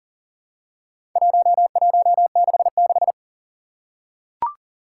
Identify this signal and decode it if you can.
Morse code, heard as a keyed tone, about 37 wpm, 700 Hz. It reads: 1166
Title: Four Challenging Letter-Number Combinations 40wpm